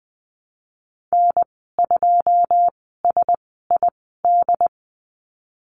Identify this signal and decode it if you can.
Morse code, heard as a keyed tone, about 20 wpm, spaced as for 15 wpm, 700 Hz. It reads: N2SID
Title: 128 Lesson 21g - Number 6 - Call Signs 20wpm